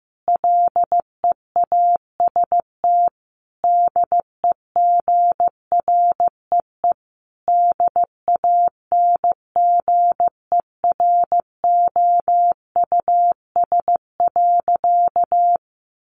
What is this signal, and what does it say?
Morse code, heard as a keyed tone, about 15 wpm, 700 Hz. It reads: LEAST DEGREE DANGEROUS.
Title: Sentences from Top 1500 Words 15wpm